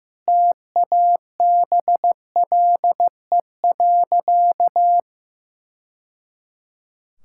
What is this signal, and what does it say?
Morse code, heard as a keyed tone, about 15 wpm, 700 Hz. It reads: TABLE.
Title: Sentences from Top 1000 Words - Review Part 5 - 15wpm